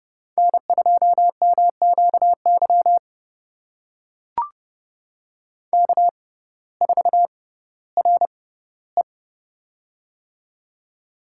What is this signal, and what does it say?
Morse code, heard as a keyed tone, about 30 wpm, spaced as for 17 wpm, 700 Hz. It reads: N2MQY K4RE
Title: USA Call Signs - 6x Character Spacing - 30wpm